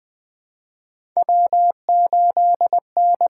Morse code, heard as a keyed tone, about 20 wpm, 700 Hz. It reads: W8N
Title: USA Call Signs - 5 Characters - 6x Character Spacing - 20wpm